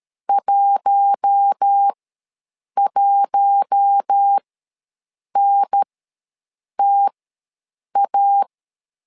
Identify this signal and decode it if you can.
Morse code, heard as a keyed tone, about 13 wpm, spaced as for 7 wpm, 800 Hz. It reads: 11NTA
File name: LEZIONI / 4 Q1 ws 12wpm 2129